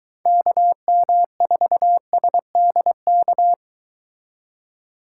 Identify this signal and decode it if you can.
Morse code, heard as a keyed tone, about 23 wpm, 700 Hz. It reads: KM4SDK